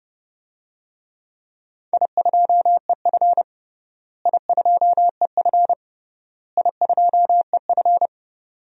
Morse code, heard as a keyed tone, about 28 wpm, 700 Hz. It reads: I2EF I2EF I2EF
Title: DXCC-100 Call Signs - 4 Characters - Repeated 3x - 30wpm